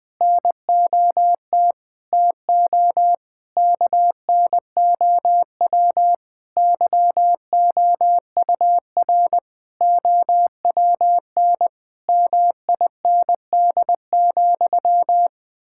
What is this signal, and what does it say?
Morse code, heard as a keyed tone, about 20 wpm, 700 Hz. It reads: NOT TO KNOW YOUR OWN MIND,
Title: Sentences from Top 1000 Words - Review Part 6 - 20wpm